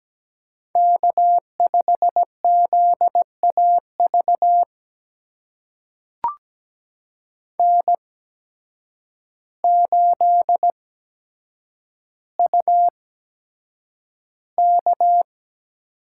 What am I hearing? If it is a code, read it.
Morse code, heard as a keyed tone, about 17 wpm, spaced as for 8 wpm, 700 Hz. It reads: K5ZAV N8UK
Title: USA Call Signs - 8x Character Spacing - 17wpm